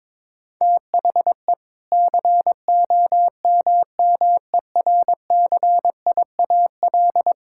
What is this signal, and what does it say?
Morse code, heard as a keyed tone, about 22 wpm, 700 Hz. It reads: THE COMMERCIAL
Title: Sentences from Top 1000 Words - Review Part 5 - 22wpm